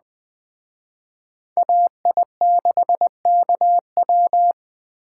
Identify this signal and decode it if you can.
Morse code, heard as a keyed tone, about 21 wpm, 700 Hz. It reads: AI6KW